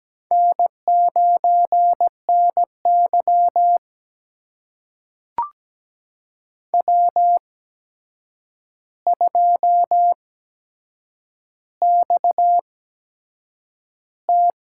Morse code, heard as a keyed tone, about 17 wpm, spaced as for 7 wpm, 700 Hz. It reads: N9NY W2XT
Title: USA Call Signs - 4 Characters - 8x Character Spacing - 17wpm